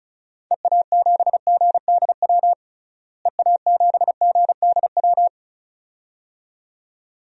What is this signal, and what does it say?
Morse code, heard as a keyed tone, about 34 wpm, 700 Hz. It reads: EA7GDW EA7GDW
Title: DXCC-100 Call Signs - 6 Characters - Repeated 2x - 35wpm